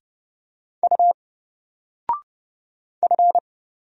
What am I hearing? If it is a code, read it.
Morse code, heard as a keyed tone, about 30 wpm, 700 Hz. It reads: U F
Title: Single Character (Letter:Number:Punctuation:Prosign) 30wpm